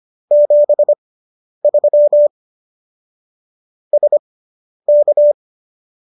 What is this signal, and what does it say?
Morse code, heard as a keyed tone, about 25 wpm, spaced as for 10 wpm, 590 Hz. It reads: 73 SK